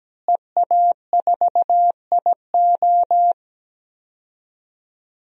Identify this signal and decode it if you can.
Morse code, heard as a keyed tone, about 17 wpm, 700 Hz. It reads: EA4IO
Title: DXCC-100 Call Signs - 5 Characters - Repeated 2x - 17wpm